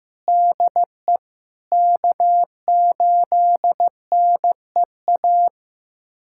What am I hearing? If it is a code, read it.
Morse code, heard as a keyed tone, about 15 wpm, 700 Hz. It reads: DE K8NEA
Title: USA Callsign DE Callsign 15wpm